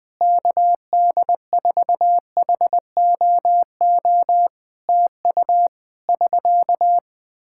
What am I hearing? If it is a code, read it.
Morse code, heard as a keyed tone, about 21 wpm, 700 Hz. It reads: KD4HOO TU <SK>
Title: Practice QSO - Exchange 4 - Ending - 20wpm